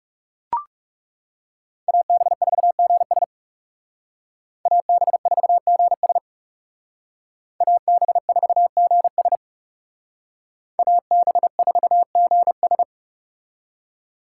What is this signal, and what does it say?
Morse code, heard as a keyed tone, about 38 wpm, 700 Hz. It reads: AB4GS AB4GS AB4GS AB4GS